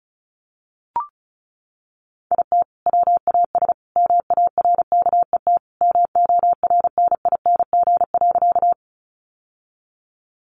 Morse code, heard as a keyed tone, about 34 wpm, 700 Hz. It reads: IT WAS MARKET MORNING.
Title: Sentences from Top 600 Words 35wpm